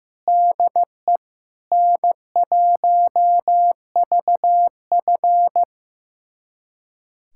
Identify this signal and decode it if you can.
Morse code, heard as a keyed tone, about 15 wpm, 700 Hz. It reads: DE N1VF
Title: USA Callsign DE Callsign - 4 Characters - 15wpm